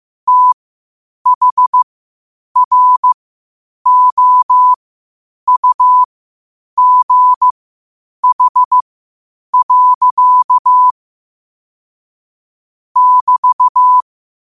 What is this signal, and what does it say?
Morse code, heard as a keyed tone, about 15 wpm, spaced as for 8 wpm, 1000 Hz. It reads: THROUGH. <BT>
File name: qotdfiles / 8wpm